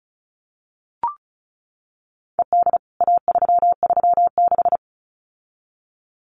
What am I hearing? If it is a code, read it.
Morse code, heard as a keyed tone, about 34 wpm, 700 Hz. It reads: ED A336
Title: CWT Exchange 35wpm